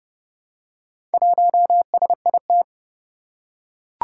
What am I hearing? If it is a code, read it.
Morse code, heard as a keyed tone, about 29 wpm, 700 Hz. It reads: 1SIT
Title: Four Challenging Letter-Number Combinations 30wpm